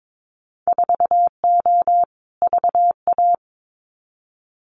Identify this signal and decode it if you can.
Morse code, heard as a keyed tone, about 22 wpm, 700 Hz. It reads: 4O VA
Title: Winter Field Day Exchanges Encore 22wpm